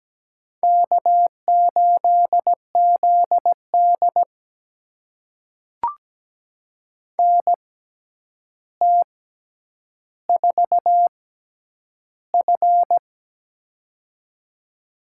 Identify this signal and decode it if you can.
Morse code, heard as a keyed tone, about 17 wpm, spaced as for 9 wpm, 700 Hz. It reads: K8ZD NT4F
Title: USA Call Signs - 4 Characters - 6x Character Spacing - 17wpm